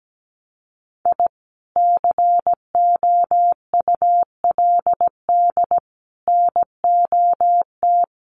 Morse code, heard as a keyed tone, about 17 wpm, 700 Hz. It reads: I COULD NOT
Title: Sets of 5 Words Encore 17wpm